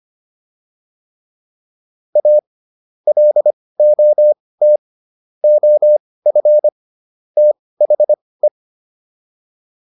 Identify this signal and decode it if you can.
Morse code, heard as a keyed tone, about 25 wpm, spaced as for 18 wpm, 600 Hz. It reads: A LOT OF THE